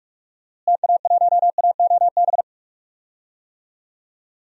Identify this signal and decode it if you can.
Morse code, heard as a keyed tone, about 47 wpm, 690 Hz. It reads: TA1AOB
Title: DXCC-100 Call Signs - 6 Characters - Repeated 2x - 45wpm